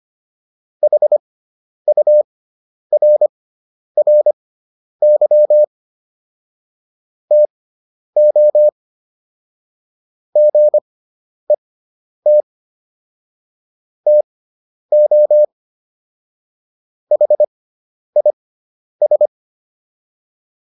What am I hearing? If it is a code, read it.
Morse code, heard as a keyed tone, about 25 wpm, spaced as for 10 wpm, 600 Hz. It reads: HURRY TO GET TO HIS